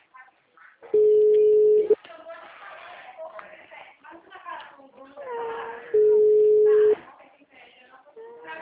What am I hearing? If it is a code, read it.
Ringback tone (Europe, 425 Hz), with speech.